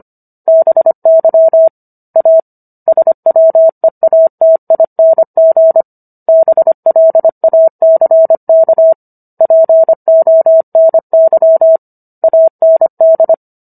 Morse code, heard as a keyed tone, about 25 wpm, 650 Hz. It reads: BY A SWEATING BLACK PONY AND